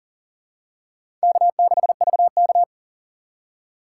Morse code, heard as a keyed tone, about 40 wpm, 700 Hz. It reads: K6VK
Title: USA Call Signs - 4 Characters - 6x Character Spacing - 40wpm